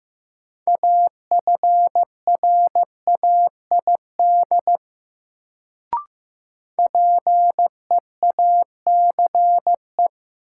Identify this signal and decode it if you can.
Morse code, heard as a keyed tone, about 15 wpm, 700 Hz. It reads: AFRAID PEACE
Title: Sentences from Top 1500 Words - New Word Drill Alt - 15wpm